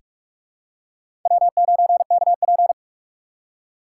Morse code, heard as a keyed tone, about 46 wpm, 700 Hz. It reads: W9KP
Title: USA Call Signs - 4 Characters - 4x Character Spacing - 45wpm